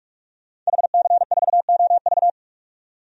Morse code, heard as a keyed tone, about 41 wpm, 700 Hz. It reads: HC4OV
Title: DXCC-100 Call Signs - 5 Characters - 45wpm